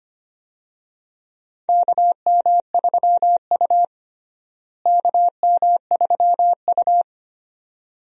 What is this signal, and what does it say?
Morse code, heard as a keyed tone, about 25 wpm, 700 Hz. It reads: KM3U KM3U
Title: USA Call Signs - 4 Characters - Repeated 2x - 25wpm